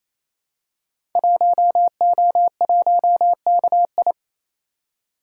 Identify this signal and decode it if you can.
Morse code, heard as a keyed tone, about 27 wpm, 700 Hz. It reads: 1O1KI
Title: Code Groups - 5 Letter:Numbers - 28wpm